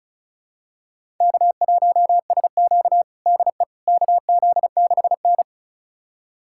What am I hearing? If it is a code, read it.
Morse code, heard as a keyed tone, about 34 wpm, 690 Hz. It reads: K1SQ DE KZ6N
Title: USA Callsign DE Callsign - 4 Characters - 35wpm